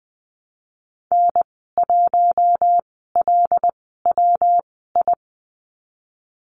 Morse code, heard as a keyed tone, about 20 wpm, spaced as for 15 wpm, 700 Hz. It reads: N1LWI
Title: 128 Lesson 21g - Number 6 - Call Signs 20wpm